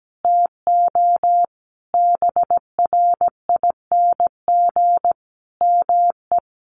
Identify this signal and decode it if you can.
Morse code, heard as a keyed tone, about 17 wpm, 700 Hz. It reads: TO BRING ME
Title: Sentences from Top 1200 Words 17wpm